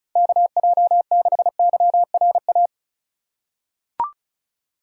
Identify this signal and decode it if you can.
Morse code, heard as a keyed tone, about 34 wpm, 700 Hz. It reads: KJ6YRA
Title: USA Call Signs - 2x Character Spacing - 35wpm